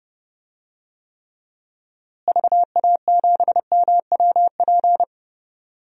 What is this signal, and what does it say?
Morse code, heard as a keyed tone, about 30 wpm, 700 Hz. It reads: VA7MWP
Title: DXCC-100 Call Signs - 6 Characters - Repeated 2x - 30wpm